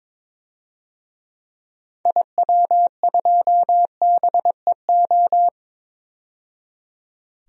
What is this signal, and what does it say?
Morse code, heard as a keyed tone, about 21 wpm, 700 Hz. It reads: IW2BEO
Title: DXCC-100 Call Signs - 6 Characters - 22wpm